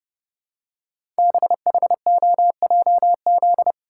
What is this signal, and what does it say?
Morse code, heard as a keyed tone, about 30 wpm, 700 Hz. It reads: BHOJZ
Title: Code Groups - 5 Letters - 30wpm